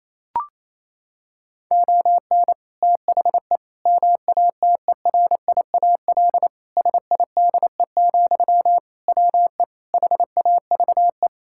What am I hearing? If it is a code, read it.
Morse code, heard as a keyed tone, about 27 wpm, 700 Hz. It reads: ON THE MATERIAL SIDE, WE HAVE
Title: Sentences from Top 1000 Words - Review Part 5 - 28wpm